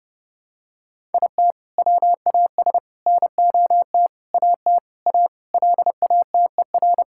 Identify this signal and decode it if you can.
Morse code, heard as a keyed tone, about 30 wpm, 700 Hz. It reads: IT WAS NOT AT A LATER
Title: Sentences from Top 1000 Words - Review Part 5 - 30wpm